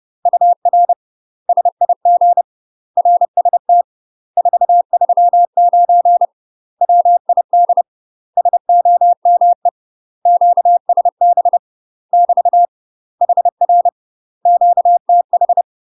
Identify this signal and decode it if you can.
Morse code, heard as a keyed tone, about 30 wpm, 700 Hz. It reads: UR SIG RST 439 WID SOME QSB <BT> HR QTH
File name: UK MASTER QSOs TRAINING / 30WPM / UK MASTER QSOs 0048